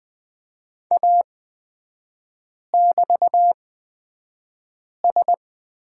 Morse code, heard as a keyed tone, about 20 wpm, spaced as for 5 wpm, 700 Hz. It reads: A<BT>S
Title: 236 Lesson 38e - Letter q - Three Characters 20wpm